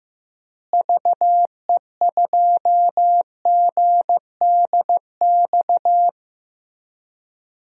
Morse code, heard as a keyed tone, about 16 wpm, 700 Hz. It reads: VE2GDX
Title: DXCC-100 Call Signs - 6 Characters - Repeated 3x - 15wpm